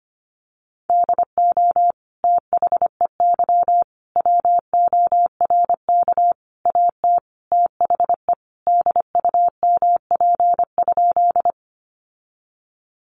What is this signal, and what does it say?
Morse code, heard as a keyed tone, about 25 wpm, 700 Hz. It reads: DO THEY WORK AT THE DUMP?
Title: Sentences from Top 1300 Words 25wpm